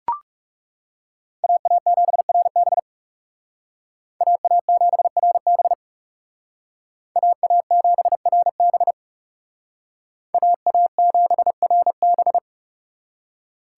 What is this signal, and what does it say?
Morse code, heard as a keyed tone, about 38 wpm, 700 Hz. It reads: AA7RB AA7RB AA7RB AA7RB